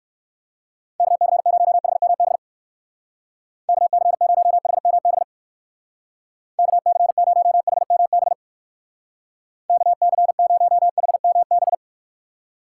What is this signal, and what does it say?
Morse code, heard as a keyed tone, about 56 wpm, 700 Hz. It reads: KC0HMB KC0HMB KC0HMB KC0HMB
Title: Speed Racing - USA 2x3 Call Signs 45wpm